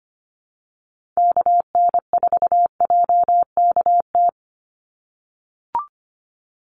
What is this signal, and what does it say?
Morse code, heard as a keyed tone, about 25 wpm, 700 Hz. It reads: KN4JKT
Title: USA Call Signs 25wpm